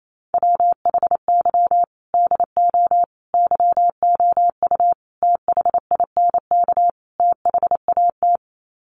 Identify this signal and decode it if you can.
Morse code, heard as a keyed tone, about 28 wpm, 700 Hz. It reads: WHY DO YOU THINK THAT
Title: Sets of 5 Words Encore2 28wpm